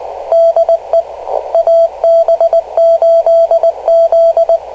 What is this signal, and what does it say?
Morse code, heard as a keyed tone, about 19 wpm, 670 Hz. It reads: DE AB8Z
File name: kl7fdq / beacons / ab8z-rst589